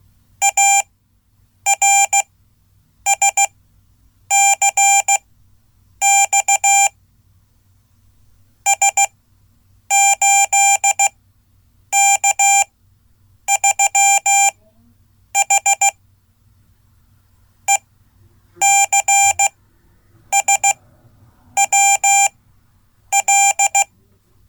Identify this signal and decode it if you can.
Morse code, heard as a keyed tone, about 15 wpm, spaced as for 8 wpm, 790 Hz. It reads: ARSCX S8K3H ECSWL
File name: Code / 5min-b